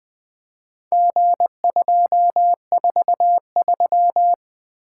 Morse code, heard as a keyed tone, about 20 wpm, 700 Hz. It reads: G243
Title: Four Letter-Number Combinations - 20wpm